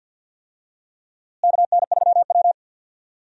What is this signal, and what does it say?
Morse code, heard as a keyed tone, about 52 wpm, 700 Hz. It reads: KN3W
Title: USA Call Signs - 4 Characters - Repeated 2x - 50wpm